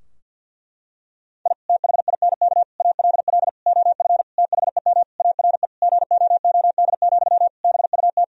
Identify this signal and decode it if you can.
Morse code, heard as a keyed tone, about 50 wpm, 700 Hz. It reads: I THINK ALL OF THEM ARE GOOD, BUT
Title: Sentences from Top 1000 Words - Review Part 4 - 50wpm